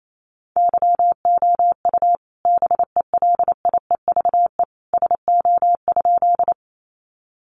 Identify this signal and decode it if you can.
Morse code, heard as a keyed tone, about 28 wpm, 700 Hz. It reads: YOU BELIEVE SO?
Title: Sentences from Top 1000 Words 28wpm